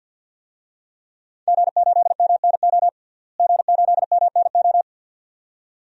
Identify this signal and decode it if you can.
Morse code, heard as a keyed tone, about 50 wpm, 700 Hz. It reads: G8MNO G8MNO
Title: DXCC-100 Call Signs - 5 Characters - Repeated 2x - 50wpm